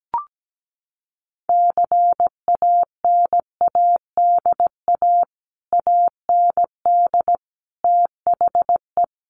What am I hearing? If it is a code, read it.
Morse code, heard as a keyed tone, about 17 wpm, 700 Hz. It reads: CANADA AND THE
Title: Sets of 5 Words Encore 17wpm